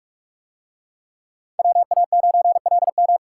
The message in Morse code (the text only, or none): WA9LM